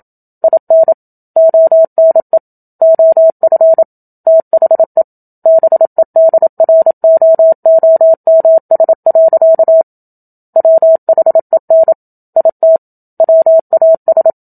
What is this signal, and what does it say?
Morse code, heard as a keyed tone, about 27 wpm, 650 Hz. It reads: IN ONE OF THE BEDROOMS. WHEN IT WAS
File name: Book Two / 01-Chapter 1 - 27 WPM